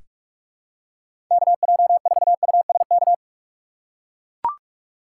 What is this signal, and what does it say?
Morse code, heard as a keyed tone, about 43 wpm, 700 Hz. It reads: KJ4USK
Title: USA Call Signs - 6 Characters - 2x Character Spacing - 45wpm